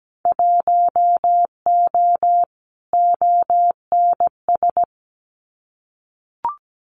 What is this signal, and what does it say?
Morse code, heard as a keyed tone, about 17 wpm, 700 Hz. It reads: 1O ONS
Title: Winter Field Day Exchanges 17wpm